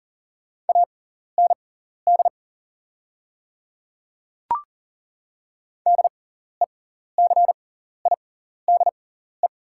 Morse code, heard as a keyed tone, about 40 wpm, spaced as for 14 wpm, 700 Hz. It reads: AND DECIDE
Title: 6x Character Spacing - Top 500 Words - 40wpm